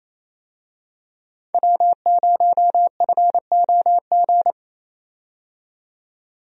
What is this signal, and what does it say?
Morse code, heard as a keyed tone, about 27 wpm, 700 Hz. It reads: W0FOG
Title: USA Call Signs - 5 Characters - 28wpm